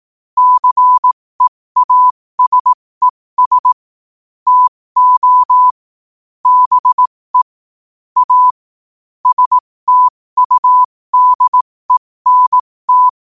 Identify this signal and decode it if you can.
Morse code, heard as a keyed tone, about 18 wpm, spaced as for 15 wpm, 1000 Hz. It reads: CEASES TO BE A STUDENT